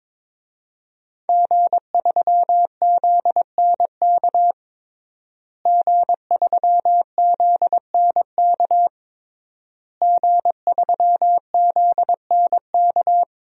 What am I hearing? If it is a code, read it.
Morse code, heard as a keyed tone, about 22 wpm, 700 Hz. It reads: G3ZNK G3ZNK G3ZNK
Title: DXCC-100 Call Signs - 5 Characters - Repeated 3x - 22wpm